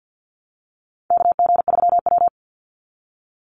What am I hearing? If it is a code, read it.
Morse code, heard as a keyed tone, about 48 wpm, 700 Hz. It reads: KG3W
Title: NA SOTA Chaser Callsigns 50wpm